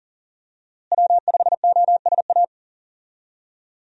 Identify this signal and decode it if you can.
Morse code, heard as a keyed tone, about 37 wpm, 700 Hz. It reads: W5OSA
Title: USA Call Signs - 5 Characters - Repeated 2x - 40wpm